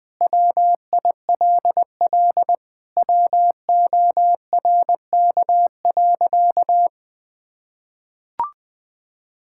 Morse code, heard as a keyed tone, about 20 wpm, 700 Hz. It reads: WILL WORK.
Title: Sentences with 8-Word Phrases - 20wpm